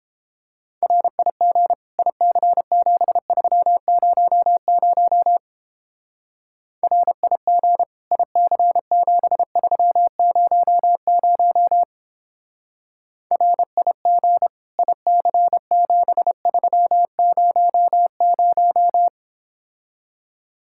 Morse code, heard as a keyed tone, about 29 wpm, 700 Hz. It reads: RIG IC7300 RIG IC7300 RIG IC7300